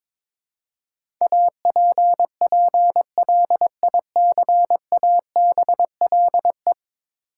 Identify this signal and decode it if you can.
Morse code, heard as a keyed tone, about 22 wpm, 700 Hz. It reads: APPLICABLE
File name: practice / 22wpm / 10 Letter Words - First 3 Letters Given - 22wpm